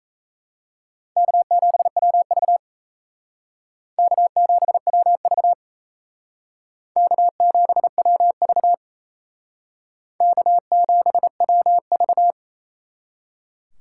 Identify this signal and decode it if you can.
Morse code, heard as a keyed tone, about 34 wpm, 700 Hz. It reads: K7WV K7WV K7WV K7WV
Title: Speed Racing - USA Call Signs 28wpm